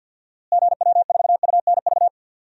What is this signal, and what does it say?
Morse code, heard as a keyed tone, about 53 wpm, 700 Hz. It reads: GW4UNV
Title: DXCC-100 Call Signs - 6 Characters - Repeated 3x - 50wpm